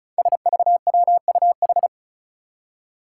Morse code, heard as a keyed tone, about 36 wpm, 700 Hz. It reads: SVWUH
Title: Five Challenging Letter-Number Combinations 35wpm